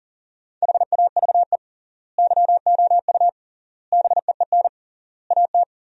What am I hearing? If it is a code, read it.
Morse code, heard as a keyed tone, about 40 wpm, 700 Hz. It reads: HAVE YOU BEEN AT
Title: 3x Word-Spacing - Sentences from Top 300 Words - 40wpm